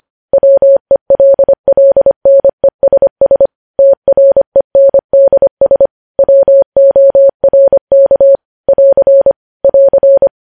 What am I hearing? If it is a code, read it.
Morse code, heard as a keyed tone, about 25 wpm, 550 Hz. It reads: WELLNESS TRENDS WORK <AR> <AR>
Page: News Headlines in Morse Code at 25 WPM